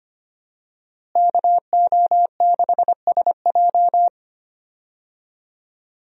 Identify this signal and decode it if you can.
Morse code, heard as a keyed tone, about 25 wpm, 700 Hz. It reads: KO6SJ
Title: USA Call Signs - 5 Characters - Repeated 2x - 25wpm